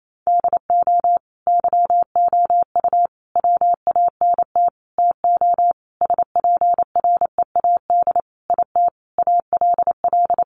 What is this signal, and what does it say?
Morse code, heard as a keyed tone, about 28 wpm, 700 Hz. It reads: DO YOU WANT TO SPREAD IT ALL
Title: Sentences from Top 1200 Words 28wpm